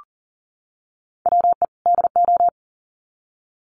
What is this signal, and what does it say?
Morse code, heard as a keyed tone, about 40 wpm, 700 Hz. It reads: WE DO
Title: Sets of 2 Words 40wpm